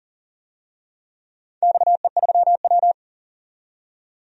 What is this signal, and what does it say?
Morse code, heard as a keyed tone, about 40 wpm, 700 Hz. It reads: XE3W
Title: DXCC-100 Call Signs - 4 Characters - 40wpm